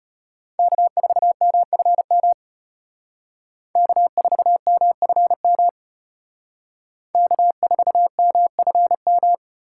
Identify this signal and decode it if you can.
Morse code, heard as a keyed tone, about 32 wpm, 700 Hz. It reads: K4MFM K4MFM K4MFM